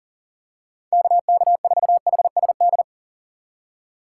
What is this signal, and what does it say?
Morse code, heard as a keyed tone, about 40 wpm, 700 Hz. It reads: KK4HSD
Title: USA Call Signs - 6 Characters - 3x Character Spacing - 40wpm